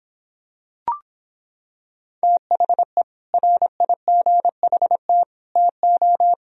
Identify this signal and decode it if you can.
Morse code, heard as a keyed tone, about 25 wpm, 700 Hz. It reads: THE RIGHT TO